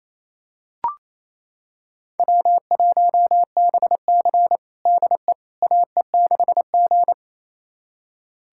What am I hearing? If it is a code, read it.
Morse code, heard as a keyed tone, about 28 wpm, 700 Hz. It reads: W1BC DE AE6G
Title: USA Callsign DE Callsign - 4 Characters - 28wpm